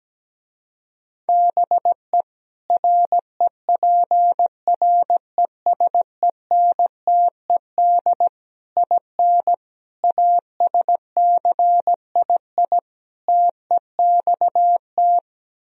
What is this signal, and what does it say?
Morse code, heard as a keyed tone, about 17 wpm, 700 Hz. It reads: BE REPRESENTED IN ASCII TEXT
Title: Speed-Racing - Sets of 5 Words 17wpm